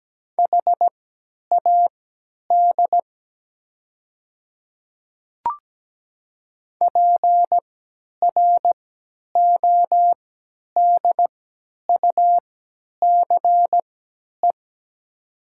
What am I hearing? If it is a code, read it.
Morse code, heard as a keyed tone, about 17 wpm, spaced as for 10 wpm, 700 Hz. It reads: HAD PRODUCE
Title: 3x Character Spacing - Top 500 Words - 17wpm